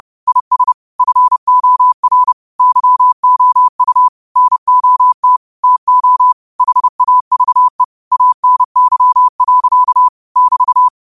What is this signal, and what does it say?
Morse code, heard as a keyed tone, about 30 wpm, 1000 Hz. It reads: IS FOR YOU NOT TO HAVE ANY. <BT>
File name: qotdfiles / 30wpm